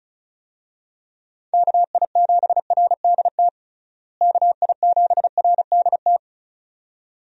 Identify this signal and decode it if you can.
Morse code, heard as a keyed tone, about 36 wpm, 700 Hz. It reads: KI7RDT KI7RDT
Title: USA Call Signs - 6 Characters - Repeated 2x - 35wpm